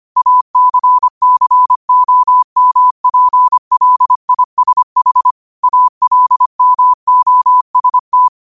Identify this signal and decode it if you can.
Morse code, heard as a keyed tone, about 25 wpm, 1000 Hz. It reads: ACCOMPLISH ALMOST
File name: qotdfiles / 25wpm